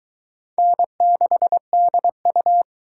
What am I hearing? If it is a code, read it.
Morse code, heard as a keyed tone, about 22 wpm, 700 Hz. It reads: N6DU